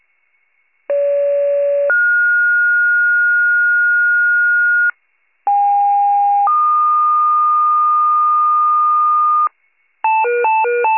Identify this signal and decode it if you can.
Two-tone page, heard as a one-second tone, then a three-second tone, about 570 then 1430 Hz, about 810 then 1220 Hz, with noise. It is heard from 0.9 to 9.5 s.